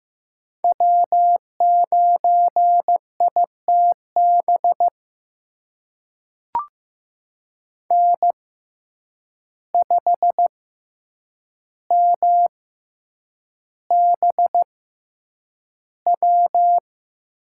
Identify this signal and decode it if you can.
Morse code, heard as a keyed tone, about 15 wpm, spaced as for 8 wpm, 700 Hz. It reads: W9ITB N5MBW
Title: USA Call Signs - 5 Characters - 6x Character Spacing - 15wpm